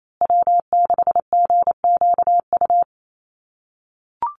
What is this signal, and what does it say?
Morse code, heard as a keyed tone, about 28 wpm, 700 Hz. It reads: W6GQU
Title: USA Call Signs Repeated 2x 28wpm